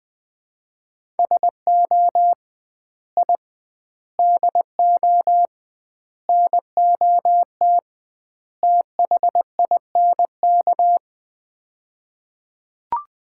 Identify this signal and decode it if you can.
Morse code, heard as a keyed tone, about 20 wpm, 700 Hz. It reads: SO I DO NOT THINK
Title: 2x Word-Spacing - Sets of 5 Words - 20wpm